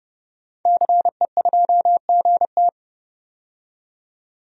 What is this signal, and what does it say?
Morse code, heard as a keyed tone, about 30 wpm, 700 Hz. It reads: CE2GT
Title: DXCC-100 Call Signs - 5 Characters - 30wpm